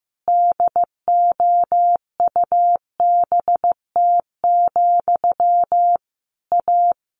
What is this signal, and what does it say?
Morse code, heard as a keyed tone, about 15 wpm, 700 Hz. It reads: DOUBT, A
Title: Sentences from Top 1500 Words 15wpm